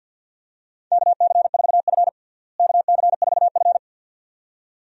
Morse code, heard as a keyed tone, about 52 wpm, 700 Hz. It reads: KC4F KC4F
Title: USA Call Signs - 4 Characters - Repeated 2x - 50wpm